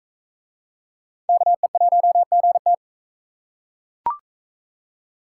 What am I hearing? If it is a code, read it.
Morse code, heard as a keyed tone, about 41 wpm, 700 Hz. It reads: KE1GT